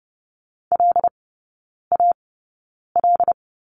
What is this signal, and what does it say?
Morse code, heard as a keyed tone, about 30 wpm, spaced as for 9 wpm, 700 Hz. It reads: LAL
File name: 081 Lesson 14e - Number 2 - Three Characters 30wpm